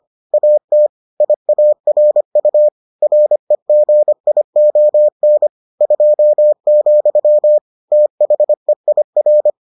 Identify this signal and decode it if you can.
Morse code, heard as a keyed tone, about 25 wpm, 600 Hz. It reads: AT IARU REGION 2, THEIR